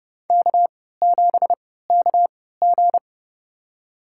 Morse code, heard as a keyed tone, about 30 wpm, spaced as for 17 wpm, 700 Hz. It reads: K7KG
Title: USA Call Signs - 4 Characters - 3x Character Spacing - 30wpm